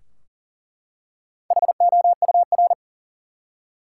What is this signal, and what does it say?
Morse code, heard as a keyed tone, about 40 wpm, 700 Hz. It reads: HOUR
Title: Top 180 QSO Elements 40wpm